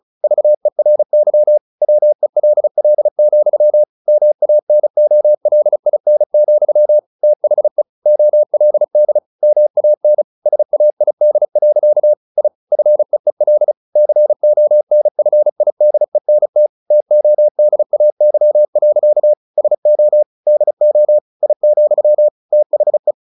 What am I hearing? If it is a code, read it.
Morse code, heard as a keyed tone, about 34 wpm, 600 Hz. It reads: VERY WELL, MANOLIN, THE OLD MAN SAID. I FEEL CONFIDENT TODAY. SO DO I, THE
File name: the old man and the sea / the old man and the sea 1 35wpm